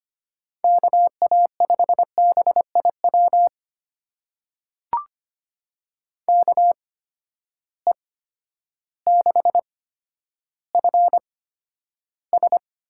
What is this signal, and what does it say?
Morse code, heard as a keyed tone, about 25 wpm, spaced as for 11 wpm, 700 Hz. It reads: KA5BIW KE6FS
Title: USA Call Signs - 6 Characters - 8x Character Spacing - 25wpm